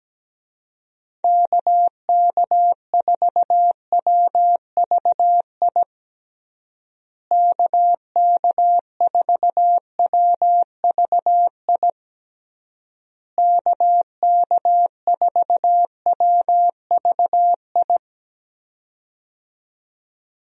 Morse code, heard as a keyed tone, about 17 wpm, 700 Hz. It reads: KK4WVI KK4WVI KK4WVI
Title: USA Call Signs - 6 Characters - Repeated 3x - 17wpm